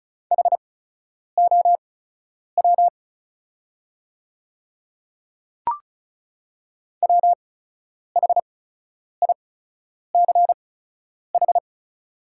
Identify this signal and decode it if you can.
Morse code, heard as a keyed tone, about 36 wpm, spaced as for 10 wpm, 690 Hz. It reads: HOW WHICH
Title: 8x Character Spacing - Top 100 Words - 35wpm